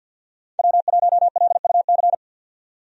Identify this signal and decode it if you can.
Morse code, heard as a keyed tone, about 50 wpm, 700 Hz. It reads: W1LUC
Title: USA Call Signs - 5 Characters - 50wpm